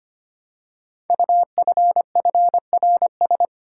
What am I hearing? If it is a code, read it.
Morse code, heard as a keyed tone, about 25 wpm, 700 Hz. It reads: UFFRS
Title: Code Groups - 5 Letters - 25wpm